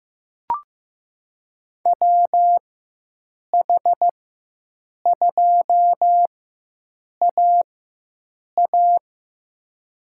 Morse code, heard as a keyed tone, about 15 wpm, spaced as for 7 wpm, 700 Hz. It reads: WH2AA
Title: USA Call Signs - 5 Characters - 4x Character Spacing - 15wpm